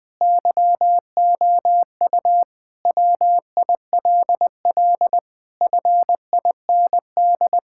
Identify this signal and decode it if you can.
Morse code, heard as a keyed tone, about 20 wpm, 700 Hz. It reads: YOU WILL FIND
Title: Sentences with 5-Word Phrases - 20wpm